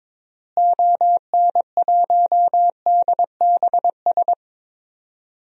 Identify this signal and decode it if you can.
Morse code, heard as a keyed tone, about 22 wpm, 700 Hz. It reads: ON1DBS
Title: DXCC-100 Call Signs - 6 Characters - Repeated 2x - 22wpm